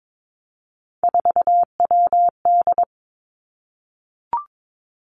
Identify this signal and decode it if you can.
Morse code, heard as a keyed tone, about 22 wpm, 700 Hz. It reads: 4WD
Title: Three Challenging Letter-Number Combinations 22wpm